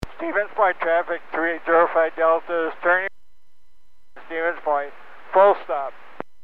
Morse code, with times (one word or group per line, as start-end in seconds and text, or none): none